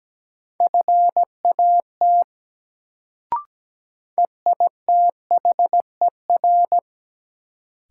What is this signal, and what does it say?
Morse code, heard as a keyed tone, about 17 wpm, 700 Hz. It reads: FAT EITHER
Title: Sentences from Top 600 Words - New Word Drill - 17wpm